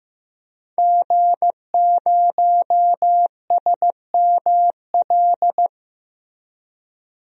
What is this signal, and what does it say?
Morse code, heard as a keyed tone, about 14 wpm, 700 Hz. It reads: G0SML
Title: DXCC-100 Call Signs - 5 Characters - Repeated 2x - 15wpm